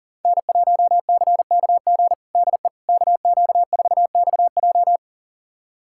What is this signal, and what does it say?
Morse code, heard as a keyed tone, about 40 wpm, 700 Hz. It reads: N1CKG DE KQ4XJ
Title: USA Callsign DE Callsign - 5 Characters - 40wpm